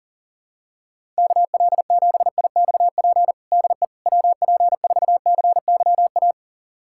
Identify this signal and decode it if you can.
Morse code, heard as a keyed tone, about 37 wpm, 700 Hz. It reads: KL7IXP DE WP4CYA
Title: USA Callsign DE Callsign - 6 Characters - 40wpm